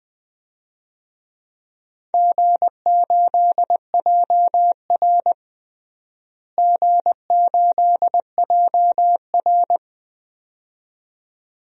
Morse code, heard as a keyed tone, about 20 wpm, 700 Hz. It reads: G8JR G8JR
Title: DXCC-100 Call Signs - 4 Characters - Repeated 2x - 20wpm